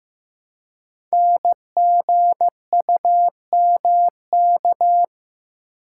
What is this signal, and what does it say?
Morse code, heard as a keyed tone, about 15 wpm, 700 Hz. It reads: NGUMK
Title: Code Groups - 5 Letters - 15wpm